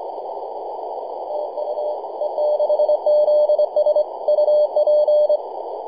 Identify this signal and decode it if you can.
Morse code, heard as a keyed tone, about 24 wpm, 600 Hz, with noise. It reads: TU E6ZS UP
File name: e6zs-21-009mhz-cw